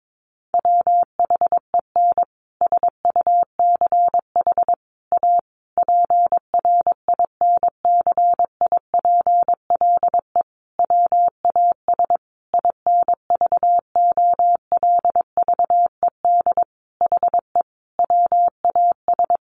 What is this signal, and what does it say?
Morse code, heard as a keyed tone, about 22 wpm, 700 Hz. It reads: WHEN SUCH A PRINCIPLE WAS INVOLVED HE WAS
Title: Sentences from Top 1400 Words 22wpm